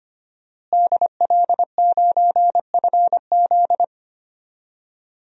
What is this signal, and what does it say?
Morse code, heard as a keyed tone, about 25 wpm, 700 Hz. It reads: DL9FZ
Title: DXCC-100 Call Signs - 5 Characters - Repeated 3x - 25wpm